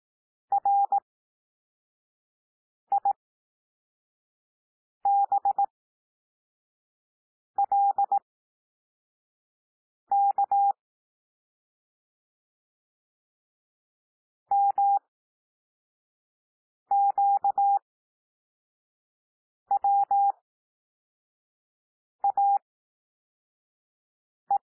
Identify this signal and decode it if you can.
Morse code, heard as a keyed tone, about 18 wpm, spaced as for 4 wpm, 800 Hz. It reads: RIBLK MQWAE
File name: cwmp3 / L2 SOUND LETTERS 18 wpm